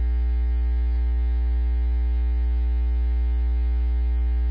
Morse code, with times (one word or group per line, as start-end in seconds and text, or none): none